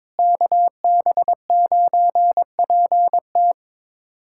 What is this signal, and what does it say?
Morse code, heard as a keyed tone, about 22 wpm, 700 Hz. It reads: KB9PT
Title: USA Callsign DE Callsign - 5 Characters - 22wpm